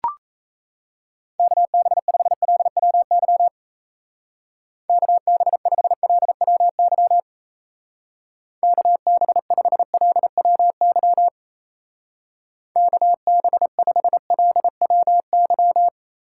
Morse code, heard as a keyed tone, about 36 wpm, 700 Hz. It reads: KB5LWY KB5LWY KB5LWY KB5LWY